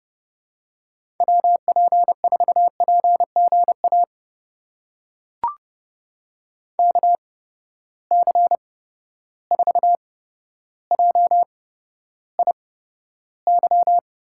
Morse code, heard as a keyed tone, about 32 wpm, spaced as for 13 wpm, 700 Hz. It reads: WP4PGA KC4JIY
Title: USA Call Signs - 6 Characters - 8x Character Spacing - 30wpm